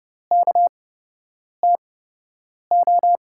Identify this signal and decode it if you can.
Morse code, heard as a keyed tone, about 29 wpm, spaced as for 8 wpm, 700 Hz. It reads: KTO
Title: USA Call Signs - 5 Characters - 8x Character Spacing - 30wpm